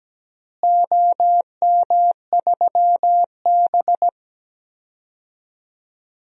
Morse code, heard as a keyed tone, about 17 wpm, 700 Hz. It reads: OM3B
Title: DXCC-100 Call Signs - 4 Characters - Repeated 2x - 17wpm